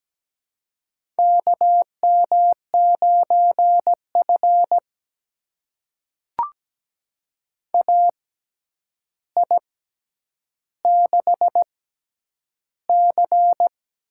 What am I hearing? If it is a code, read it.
Morse code, heard as a keyed tone, about 17 wpm, spaced as for 9 wpm, 700 Hz. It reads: KM9F AI6C
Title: USA Call Signs - 4 Characters - 6x Character Spacing - 17wpm